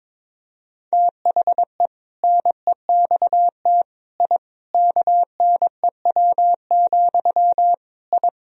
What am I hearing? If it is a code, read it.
Morse code, heard as a keyed tone, about 22 wpm, 700 Hz. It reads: THE NEXT I KNEW, I
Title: Sentences from Top 1000 Words - Review Part 3 - 22wpm